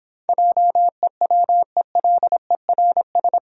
Morse code, heard as a keyed tone, about 26 wpm, 700 Hz. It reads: JEWELERS